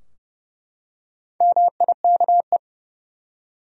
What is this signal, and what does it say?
Morse code, heard as a keyed tone, about 30 wpm, 700 Hz. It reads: MIKE
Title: USA Common Names - Max 5 Letters - 30wpm